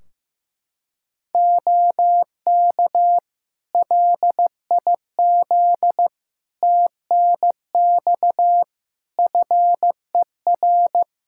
Morse code, heard as a keyed tone, about 15 wpm, 700 Hz. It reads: OK LIZ TNX FER
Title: K1USN SST QSO Practice 15wpm